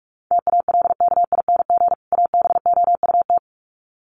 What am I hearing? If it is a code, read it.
Morse code, heard as a keyed tone, about 44 wpm, 700 Hz. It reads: TALKING ABOUT
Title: Sets of 2 Words 45wpm